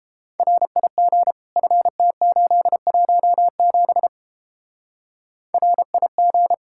Morse code, heard as a keyed tone, about 31 wpm, 700 Hz. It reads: RIG FT817 RIG